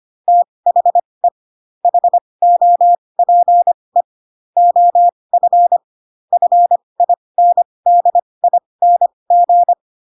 Morse code, heard as a keyed tone, about 25 wpm, spaced as for 20 wpm, 700 Hz. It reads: THE HOPE OF FINDING